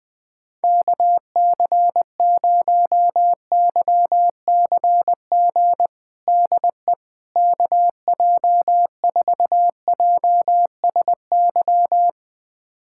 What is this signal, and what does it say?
Morse code, heard as a keyed tone, about 20 wpm, 700 Hz. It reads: KC0YCG DE KJ4JSY
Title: USA Callsign DE Callsign - 6 Characters - 20wpm